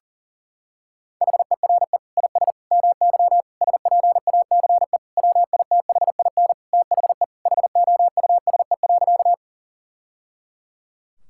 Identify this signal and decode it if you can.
Morse code, heard as a keyed tone, about 40 wpm, 700 Hz. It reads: HERE IS MY SPACE WITHIN THE HOUSE.
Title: Sentences from Top 1000 Words - Review Part 5 - 40wpm